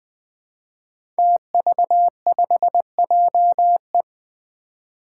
Morse code, heard as a keyed tone, about 21 wpm, 700 Hz. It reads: TV5JE
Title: Five Challenging Letter-Number Combinations 20wpm